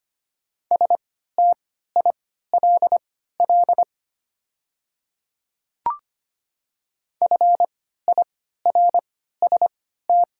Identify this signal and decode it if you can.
Morse code, heard as a keyed tone, about 25 wpm, spaced as for 14 wpm, 700 Hz. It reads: STILL FIRST
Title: 3x Character Spacing - Top 300 Words - 25wpm